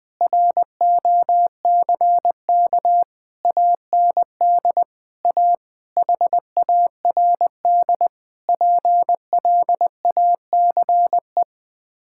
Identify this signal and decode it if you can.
Morse code, heard as a keyed tone, about 20 wpm, 700 Hz. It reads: ROCK AND A HARD PLACE
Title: Sets of 8 Words - 20wpm